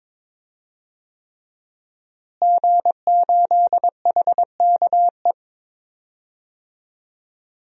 Morse code, heard as a keyed tone, about 22 wpm, 700 Hz. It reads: G8HKE
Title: DXCC-100 Call Signs - 5 Characters - Repeated 3x - 22wpm